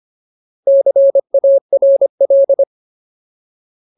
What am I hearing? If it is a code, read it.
Morse code, heard as a keyed tone, about 25 wpm, 550 Hz. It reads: CARL
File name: Ultimate Word List / 25WPM Ultimate